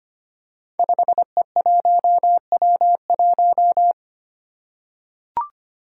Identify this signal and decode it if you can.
Morse code, heard as a keyed tone, about 25 wpm, 700 Hz. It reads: 5E1W1
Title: Five Challenging Letter-Number Combinations 25wpm